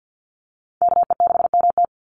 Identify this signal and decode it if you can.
Morse code, heard as a keyed tone, about 50 wpm, 700 Hz. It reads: KE6MT
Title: NA SOTA Chaser Callsigns 50wpm